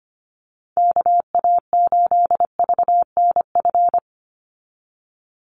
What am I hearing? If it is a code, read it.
Morse code, heard as a keyed tone, about 25 wpm, 700 Hz. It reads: KA8VNF
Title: USA Call Signs 25wpm